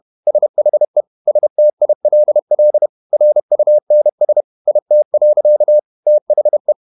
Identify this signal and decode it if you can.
Morse code, heard as a keyed tone, about 31 wpm, 600 Hz. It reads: SHE STILL RUNS IT. THE
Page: Never 31wpm | CW med Gnister